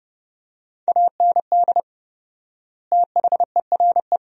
Morse code, heard as a keyed tone, about 32 wpm, 700 Hz. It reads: AND THERE
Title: 4x Word-Spacing - Sentences from Top 300 Words - 30wpm